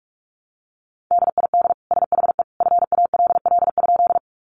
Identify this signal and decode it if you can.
Morse code, heard as a keyed tone, about 44 wpm, 700 Hz. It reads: DID SHE FALL?
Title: Sentences from Top 500 Words 45wpm